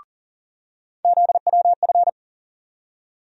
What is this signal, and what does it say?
Morse code, heard as a keyed tone, about 39 wpm, 700 Hz. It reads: ZWF
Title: USA Call Sign Postfixes - 3 Letters 40wpm